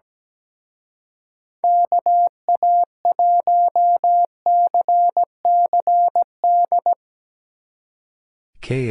Morse code, heard as a keyed tone, about 17 wpm, 700 Hz. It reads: KA1CCD
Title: Speed Racing - USA 2x3 Call Signs 17wpm